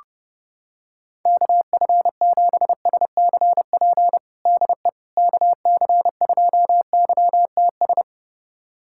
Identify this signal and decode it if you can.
Morse code, heard as a keyed tone, about 30 wpm, 700 Hz. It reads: KF7SCP DE KC2YTS
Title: USA Callsign DE Callsign - 6 Characters - 30wpm